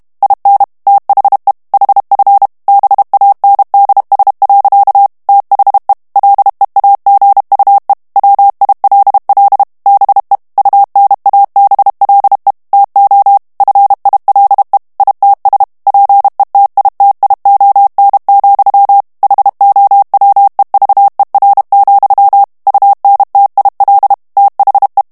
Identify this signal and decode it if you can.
Morse code, heard as a keyed tone, about 32 wpm, 800 Hz. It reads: IN THE HF BANDS. THE LEAGUE WILL BE UNABLE TO FILE ITS PETITION, HOWEVER, UNTIL THE
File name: ANGLAIS texte clair 3 vitesse 30 mots minute